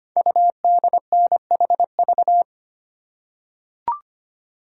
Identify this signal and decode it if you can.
Morse code, heard as a keyed tone, about 25 wpm, 700 Hz. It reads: UDNHV
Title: Code Groups - 5 Letters - 25wpm